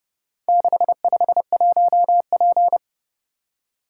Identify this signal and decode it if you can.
Morse code, heard as a keyed tone, about 27 wpm, 700 Hz. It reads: 651P